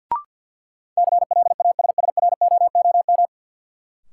Morse code, heard as a keyed tone, about 51 wpm, 700 Hz. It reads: CLASSROOM
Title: Sentences from Top 1500 Words - New Word Drill Alt - 50wpm